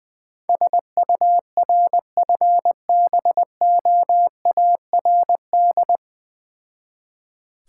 Morse code, heard as a keyed tone, about 19 wpm, 700 Hz. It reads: SURFBOARD
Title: Difficult Words with Hints - 20wpm